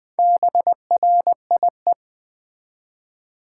Morse code, heard as a keyed tone, about 20 wpm, 700 Hz. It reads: BRIE
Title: Build A Word - 1 Letter At A Time - 5-Letter Words - 20wpm